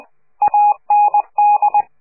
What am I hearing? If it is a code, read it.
Morse code, heard as a keyed tone, about 20 wpm, 830 Hz. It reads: AND
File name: CQ de PCH laatste bericht farewell